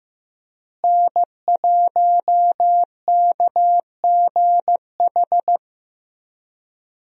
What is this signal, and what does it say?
Morse code, heard as a keyed tone, about 15 wpm, 700 Hz. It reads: N1KGH
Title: USA Call Signs - 5 Characters - Repeated 2x - 15wpm